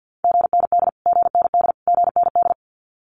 Morse code, heard as a keyed tone, about 50 wpm, 700 Hz. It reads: GND GND GND
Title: Abbreviations Repeated 3x - 50wpm